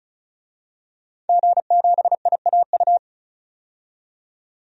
Morse code, heard as a keyed tone, about 36 wpm, 700 Hz. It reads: G7IAU
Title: DXCC-100 Call Signs - 5 Characters - 35wpm